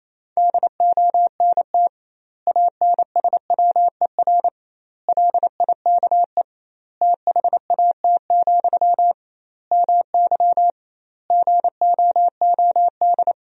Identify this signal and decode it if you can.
Morse code, heard as a keyed tone, about 28 wpm, 700 Hz. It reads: DONT ANSWER LIKE THAT, MY GOOD
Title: 2x Word-Spacing - Sentences from Top 400 Words - 28wpm